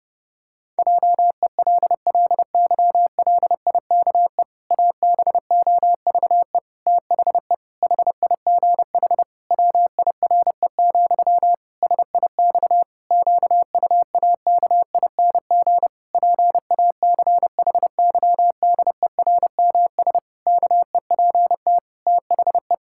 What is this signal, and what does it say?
Morse code, heard as a keyed tone, about 30 wpm, 700 Hz. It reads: JELLYLIKE ABOVE THE HIGH WIRE, SIX QUAKING PACHYDERMS KEPT THE
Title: Pangram Challenge v2 - 30wpm